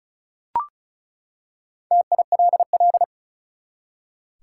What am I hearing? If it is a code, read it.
Morse code, heard as a keyed tone, about 34 wpm, 700 Hz. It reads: TILL
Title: Four Letter Words - 35wpm